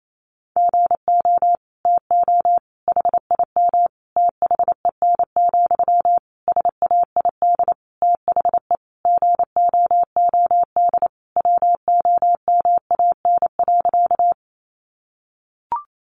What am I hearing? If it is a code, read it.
Morse code, heard as a keyed tone, about 28 wpm, 700 Hz. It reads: GO TO HIM THEN, SAID THE GOOD WOMAN.
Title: Sentences from Top 1100 Words 28wpm